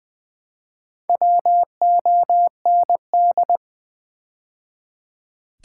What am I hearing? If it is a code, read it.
Morse code, heard as a keyed tone, about 20 wpm, 700 Hz. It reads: WOND
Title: Build A Word - 2 Letters At A Time - 6-Letter Words - 20wpm